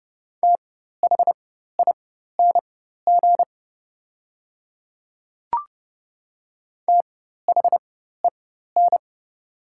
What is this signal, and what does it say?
Morse code, heard as a keyed tone, about 30 wpm, spaced as for 14 wpm, 690 Hz. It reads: THING THEN
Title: 4x Character Spacing - Top 100 Words - 30wpm